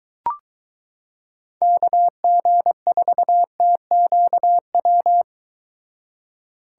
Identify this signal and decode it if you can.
Morse code, heard as a keyed tone, about 22 wpm, 700 Hz. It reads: KG4TQW